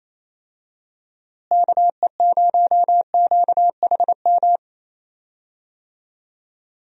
Morse code, heard as a keyed tone, about 28 wpm, 700 Hz. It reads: KE0QHM
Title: USA Call Signs - 6 Characters - Repeated 2x - 28wpm